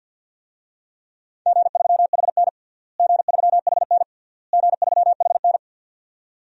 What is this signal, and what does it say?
Morse code, heard as a keyed tone, about 53 wpm, 700 Hz. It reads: G3HN G3HN G3HN
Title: DXCC-100 Call Signs - 4 Characters - Repeated 3x - 50wpm